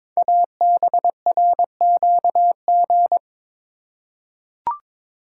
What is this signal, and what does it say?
Morse code, heard as a keyed tone, about 22 wpm, 700 Hz. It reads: ABRQG
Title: Code Groups - 5 Letters - 22wpm